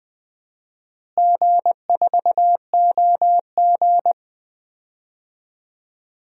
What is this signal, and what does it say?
Morse code, heard as a keyed tone, about 20 wpm, 700 Hz. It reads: G4OG
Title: DXCC-100 Call Signs - 4 Characters - Repeated 3x - 20wpm